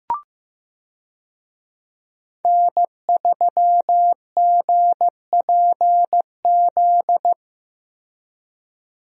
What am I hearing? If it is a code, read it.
Morse code, heard as a keyed tone, about 15 wpm, 700 Hz. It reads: N3GPZ
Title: USA Call Signs - 5 Characters - Repeated 2x - 15wpm